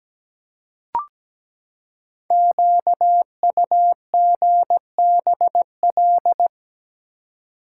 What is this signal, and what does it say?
Morse code, heard as a keyed tone, about 17 wpm, 700 Hz. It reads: QUGBL
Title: Code Groups - 5 Letters - 17wpm